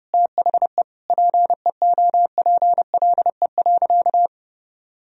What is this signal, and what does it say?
Morse code, heard as a keyed tone, about 30 wpm, 700 Hz. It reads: THE PEOPLE.
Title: Sentences from Top 400 Words and Abbreviations 30wpm